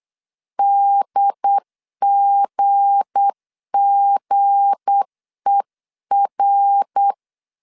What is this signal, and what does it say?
Morse code, heard as a keyed tone, about 8 wpm, 800 Hz. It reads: DGGER
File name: LEZIONI / 7 GF9Z / 7 G 8wpm 3118